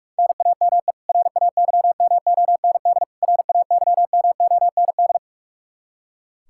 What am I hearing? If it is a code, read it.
Morse code, heard as a keyed tone, about 46 wpm, 700 Hz. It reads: NAME RAYMOND RAYMOND
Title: Practice QSO - Exchange 1 - Beginning - 45wpm